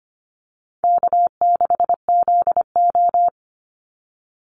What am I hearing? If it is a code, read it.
Morse code, heard as a keyed tone, about 25 wpm, 700 Hz. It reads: K6ZO
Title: Easy Call Signs Repeated 2x - 25wpm